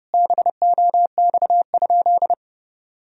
Morse code, heard as a keyed tone, about 30 wpm, 700 Hz. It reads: BOX?
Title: Sentences from Top 1000 Words - Review Part 4 - 30wpm